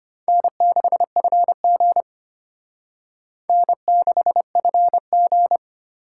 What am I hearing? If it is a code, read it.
Morse code, heard as a keyed tone, about 33 wpm, 700 Hz. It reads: N6FG N6FG